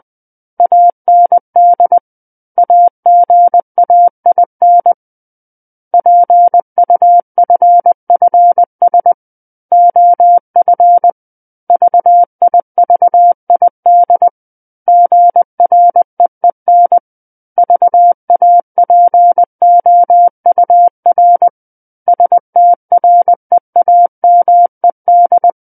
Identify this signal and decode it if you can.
Morse code, heard as a keyed tone, about 20 wpm, 700 Hz. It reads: AND AGAIN PUFFS OF VIVID GREEN VAPOUR STREAMED
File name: 11-Chapter 11 - 20 WPM